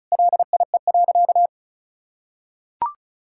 Morse code, heard as a keyed tone, about 36 wpm, 700 Hz. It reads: LIE.
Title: Sentences from Top 1600 Words 35wpm